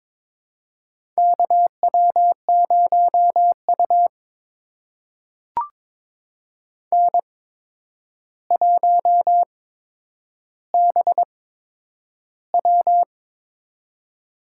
Morse code, heard as a keyed tone, about 22 wpm, spaced as for 9 wpm, 700 Hz. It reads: KW0U N1BW
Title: USA Call Signs - 4 Characters - 8x Character Spacing - 22wpm